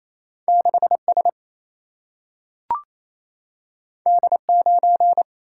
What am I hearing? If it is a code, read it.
Morse code, heard as a keyed tone, about 28 wpm, 700 Hz. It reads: 6 S D 9
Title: Single-Letter Single-Number 28wpm